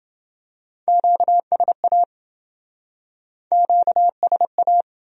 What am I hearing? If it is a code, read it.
Morse code, heard as a keyed tone, about 28 wpm, 700 Hz. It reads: QSA QSA